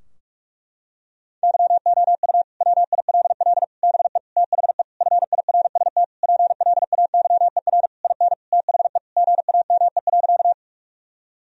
Morse code, heard as a keyed tone, about 46 wpm, 700 Hz. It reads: YOU WILL BE THE FIRST PLAYER IN THE GAME.
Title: Sentences from Top 1000 Words - Review Part 4 - 45wpm